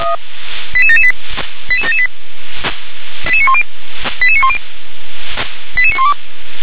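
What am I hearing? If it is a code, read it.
Five-tone selective call (ZVEI), heard as quick runs of five tones, tones of about 70 ms. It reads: 88778 88778 88910 88910 88912
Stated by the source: ZVEI five-tone selective call received on an SDR.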